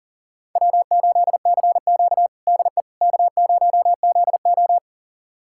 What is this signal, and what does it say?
Morse code, heard as a keyed tone, about 40 wpm, 700 Hz. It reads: W8CQ DE K0ZO
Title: USA Callsign DE Callsign - 4 Characters - 40wpm